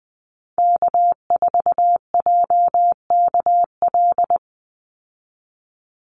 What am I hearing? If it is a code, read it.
Morse code, heard as a keyed tone, about 20 wpm, 700 Hz. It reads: K4JKL
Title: Call Signs Repeated 3x - 20wpm